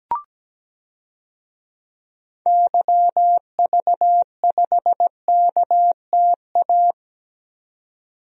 Morse code, heard as a keyed tone, about 17 wpm, 700 Hz. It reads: YV5KTA
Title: DXCC-100 Call Signs - 6 Characters - 17wpm